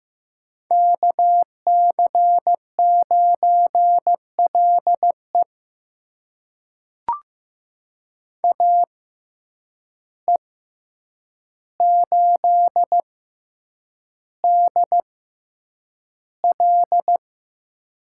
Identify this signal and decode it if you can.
Morse code, heard as a keyed tone, about 15 wpm, spaced as for 8 wpm, 700 Hz. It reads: KC9LE AE8DL
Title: USA Call Signs - 5 Characters - 6x Character Spacing - 15wpm